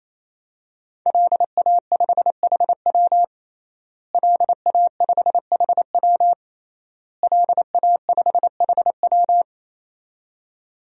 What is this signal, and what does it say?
Morse code, heard as a keyed tone, about 28 wpm, 700 Hz. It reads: LA5HW LA5HW LA5HW
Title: DXCC-100 Call Signs - 5 Characters - Repeated 3x - 28wpm